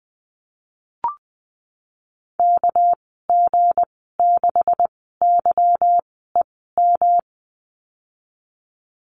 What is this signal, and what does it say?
Morse code, heard as a keyed tone, about 20 wpm, spaced as for 14 wpm, 700 Hz. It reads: KG6YEM
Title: 200 Lesson 32g - Letter k - Call Signs 20wpm